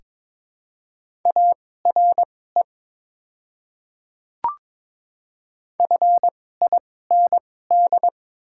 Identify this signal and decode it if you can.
Morse code, heard as a keyed tone, about 22 wpm, spaced as for 16 wpm, 690 Hz. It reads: ARE FIND
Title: 2x Character Spacing - Top 200 Words - 22wpm